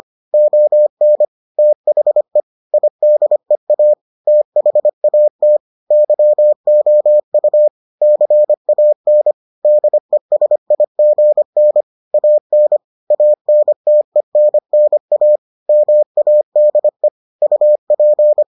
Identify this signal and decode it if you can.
Morse code, heard as a keyed tone, about 25 wpm, 600 Hz. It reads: ON THE IDEA THAT YOU CAN DESIGN AN ANTENNA MADE UP